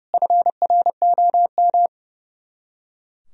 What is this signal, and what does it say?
Morse code, heard as a keyed tone, about 32 wpm, 700 Hz. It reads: FROM
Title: Top 100 Words Repeated 30wpm